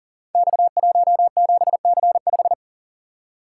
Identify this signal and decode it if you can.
Morse code, heard as a keyed tone, about 39 wpm, 700 Hz. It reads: X17C5
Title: Code Groups - 5 Letter:Numbers - 40wpm